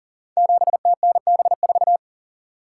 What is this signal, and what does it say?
Morse code, heard as a keyed tone, about 40 wpm, 700 Hz. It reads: 7TNB4
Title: Code Groups - 5 Letter:Numbers - 40wpm